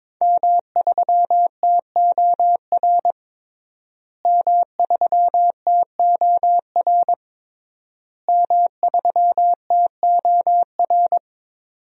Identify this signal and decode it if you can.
Morse code, heard as a keyed tone, about 22 wpm, 700 Hz. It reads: M3TOR M3TOR M3TOR
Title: DXCC-100 Call Signs - 5 Characters - Repeated 3x - 22wpm